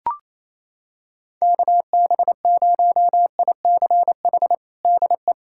Morse code, heard as a keyed tone, about 28 wpm, 700 Hz. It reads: KB0ICH DE
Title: USA Callsign DE Callsign - 6 Characters - 28wpm